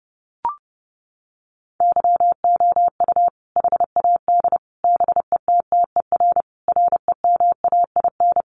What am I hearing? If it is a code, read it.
Morse code, heard as a keyed tone, about 30 wpm, 700 Hz. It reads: YOU HAD BETTER REMAIN
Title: Sentences from Top 1200 Words 30wpm